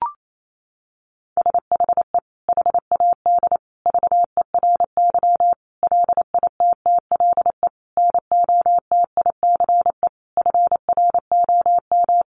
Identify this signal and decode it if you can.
Morse code, heard as a keyed tone, about 28 wpm, 700 Hz. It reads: SHE HAD VERY LITTLE NOTICE FROM
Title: Sentences from Top 400 Words 28wpm